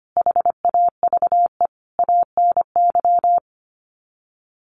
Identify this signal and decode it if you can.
Morse code, heard as a keyed tone, about 25 wpm, 700 Hz. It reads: HAVE ANY
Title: Sets of 5 Words Encore 25wpm